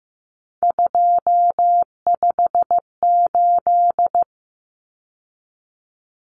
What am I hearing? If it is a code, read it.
Morse code, heard as a keyed tone, about 15 wpm, 700 Hz. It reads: 258
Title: Numbers - 3-Digit 15wpm